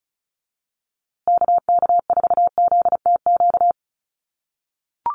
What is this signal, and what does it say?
Morse code, heard as a keyed tone, about 34 wpm, 700 Hz. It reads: KK4ZTQ
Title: USA Call Signs Repeated 3x 35wpm